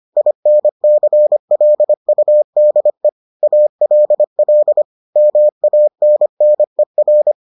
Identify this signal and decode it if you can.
Morse code, heard as a keyed tone, about 25 wpm, 600 Hz. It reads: INCLUDE ALL MANNER